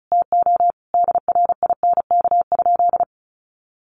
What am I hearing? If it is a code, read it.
Morse code, heard as a keyed tone, about 34 wpm, 700 Hz. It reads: TO DRINK?
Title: Sentences from Top 1500 Words 35wpm